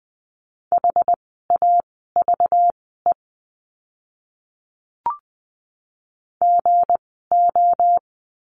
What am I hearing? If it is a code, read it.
Morse code, heard as a keyed tone, about 20 wpm, spaced as for 14 wpm, 700 Hz. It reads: HAVE GO
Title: 244 Lesson 39f - Letter z - Words 20wpm